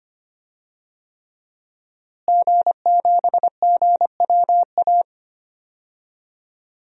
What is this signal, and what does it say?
Morse code, heard as a keyed tone, about 25 wpm, 700 Hz. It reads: G7GWA
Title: DXCC-100 Call Signs - 5 Characters - Repeated 3x - 25wpm